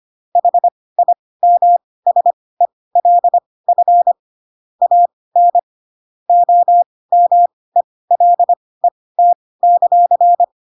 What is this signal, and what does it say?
Morse code, heard as a keyed tone, about 25 wpm, spaced as for 18 wpm, 700 Hz. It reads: HIMSELF AN OMELET;
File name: Pinocchio / Pinocchio ch5 12 25 ik1hge